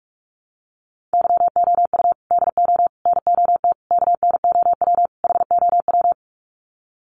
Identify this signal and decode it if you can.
Morse code, heard as a keyed tone, about 44 wpm, 700 Hz. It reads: YOU DO NOT KNOW HOW
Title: Sets of 5 Words Encore 45wpm